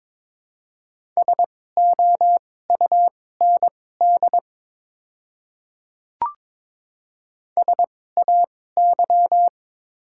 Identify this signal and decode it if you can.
Morse code, heard as a keyed tone, about 22 wpm, spaced as for 16 wpm, 700 Hz. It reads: SOUND SAY
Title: 2x Character Spacing - Top 200 Words - 22wpm